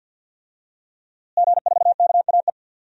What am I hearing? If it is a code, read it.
Morse code, heard as a keyed tone, about 52 wpm, 700 Hz. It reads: G4KAE
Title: DXCC-100 Call Signs - 5 Characters - Repeated 2x - 50wpm